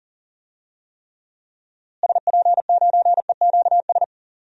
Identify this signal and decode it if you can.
Morse code, heard as a keyed tone, about 40 wpm, 700 Hz. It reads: SP9EQS
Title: DXCC-100 Call Signs - 6 Characters - Repeated 3x - 40wpm